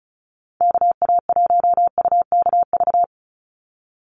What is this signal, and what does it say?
Morse code, heard as a keyed tone, about 34 wpm, 700 Hz. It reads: KA1UKV
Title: USA Call Signs 35wpm